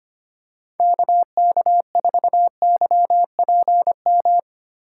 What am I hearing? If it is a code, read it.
Morse code, heard as a keyed tone, about 25 wpm, 700 Hz. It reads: KK4YPM
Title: USA Call Signs - 6 Characters - 3x Character Spacing - 25wpm